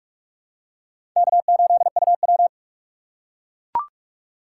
Morse code, heard as a keyed tone, about 44 wpm, 700 Hz. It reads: K8UW
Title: USA Call Signs - 4 Characters - 6x Character Spacing - 45wpm